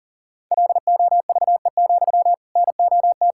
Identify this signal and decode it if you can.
Morse code, heard as a keyed tone, about 40 wpm, 700 Hz. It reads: LOVE, NOT
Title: Sentences from Top 1600 Words 40wpm